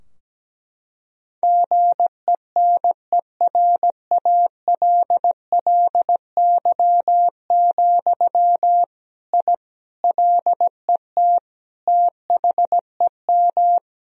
Morse code, heard as a keyed tone, about 17 wpm, 700 Hz. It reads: GENERALLY, I LET THEM
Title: Sentences from Top 1000 Words - Review Part 4 - 17wpm